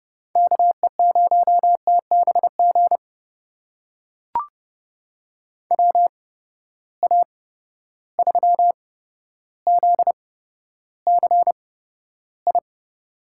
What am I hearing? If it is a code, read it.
Morse code, heard as a keyed tone, about 32 wpm, spaced as for 13 wpm, 700 Hz. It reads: KE0TBG WA3ZCI
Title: USA Call Signs - 6 Characters - 8x Character Spacing - 30wpm